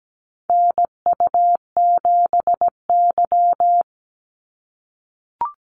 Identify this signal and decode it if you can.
Morse code, heard as a keyed tone, about 17 wpm, 700 Hz. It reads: NU7Y
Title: NA SOTA Activator Callsigns 17wpm